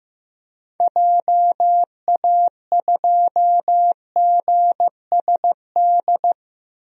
Morse code, heard as a keyed tone, about 15 wpm, 700 Hz. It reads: JA2GSD
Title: DXCC-100 Call Signs - 6 Characters - Repeated 2x - 15wpm